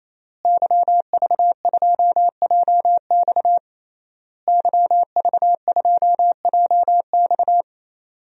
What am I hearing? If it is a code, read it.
Morse code, heard as a keyed tone, about 28 wpm, 700 Hz. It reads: YV2JX YV2JX
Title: DXCC-100 Call Signs - 5 Characters - Repeated 3x - 28wpm